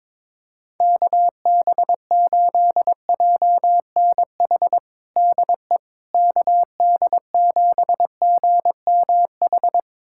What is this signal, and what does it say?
Morse code, heard as a keyed tone, about 23 wpm, 700 Hz. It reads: KB8JNH DE KD7GMH
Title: USA Callsign DE Callsign - 6 Characters - 22wpm